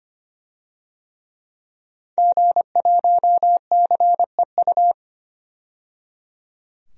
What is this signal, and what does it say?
Morse code, heard as a keyed tone, about 25 wpm, 700 Hz. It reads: G1CEU
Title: DXCC-100 Call Signs - 5 Characters - 25wpm